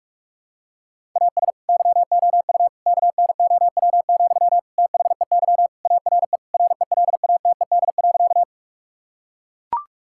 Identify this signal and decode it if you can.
Morse code, heard as a keyed tone, about 42 wpm, 700 Hz. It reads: AS YOU KNOW, THEY ARE RELATED.
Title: Sentences from Top 1000 Words - Review Part 5 - 45wpm